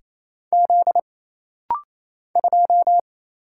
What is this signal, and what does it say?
Morse code, heard as a keyed tone, about 27 wpm, 700 Hz. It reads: Z 2
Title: Single Letter-Number - Rapid-Fire 28wpm